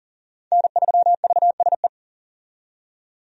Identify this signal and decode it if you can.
Morse code, heard as a keyed tone, about 40 wpm, 700 Hz. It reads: N3VSE
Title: USA Call Signs - 5 Characters - 40wpm